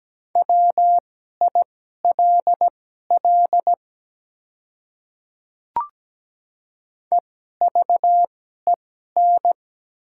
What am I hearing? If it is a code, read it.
Morse code, heard as a keyed tone, about 17 wpm, spaced as for 12 wpm, 700 Hz. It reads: WILL EVEN
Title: 2x Character Spacing - Top 300 Words - 17wpm